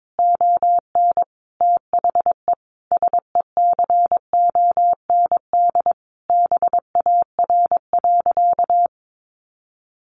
Morse code, heard as a keyed tone, about 22 wpm, 700 Hz. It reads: ON THE SECOND BAR.
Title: Easy Phrases 22wpm